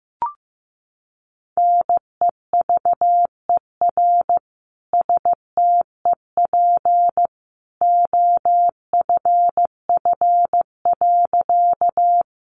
Morse code, heard as a keyed tone, about 15 wpm, 700 Hz. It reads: NEVER STEP OFF.
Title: A Sound Of Thunder 15wpm